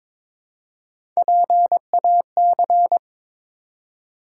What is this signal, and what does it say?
Morse code, heard as a keyed tone, about 22 wpm, 700 Hz. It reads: PAC
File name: practice / 22wpm / ARRL Sections v2 - 22wpm